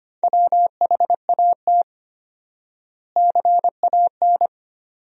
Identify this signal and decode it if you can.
Morse code, heard as a keyed tone, about 25 wpm, 700 Hz. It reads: WHAT CAN
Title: 4x Word-Spacing - Sentences from Top 400 Words - 25wpm